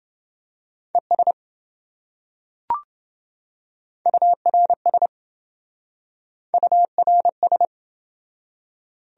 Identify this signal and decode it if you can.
Morse code, heard as a keyed tone, about 27 wpm, 700 Hz. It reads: ES URS URS